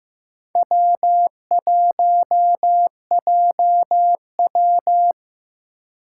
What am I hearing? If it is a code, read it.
Morse code, heard as a keyed tone, about 15 wpm, 700 Hz. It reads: W1JW
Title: USA Call Signs - 4 Characters - 3x Character Spacing - 15wpm